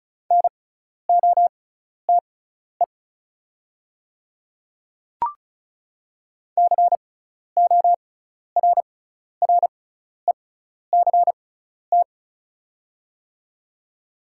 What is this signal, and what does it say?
Morse code, heard as a keyed tone, about 36 wpm, spaced as for 12 wpm, 700 Hz. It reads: NOTE CORRECT
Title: 6x Character Spacing - Top 500 Words - 35wpm